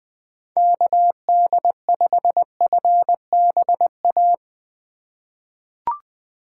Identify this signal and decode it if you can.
Morse code, heard as a keyed tone, about 21 wpm, 700 Hz. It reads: KD5FBA
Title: K1USN SST QSO Practice - 20wpm